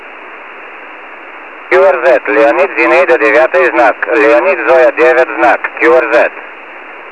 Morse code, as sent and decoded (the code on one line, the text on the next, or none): none